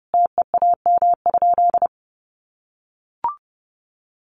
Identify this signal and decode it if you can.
Morse code, heard as a keyed tone, about 30 wpm, 700 Hz. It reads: TEAM?
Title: Sentences from Top 1500 Words 30wpm